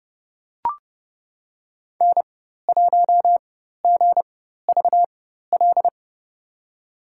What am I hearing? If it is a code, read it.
Morse code, heard as a keyed tone, about 30 wpm, spaced as for 14 wpm, 700 Hz. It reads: N1GVL
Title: USA Call Signs - 5 Characters - 4x Character Spacing - 30wpm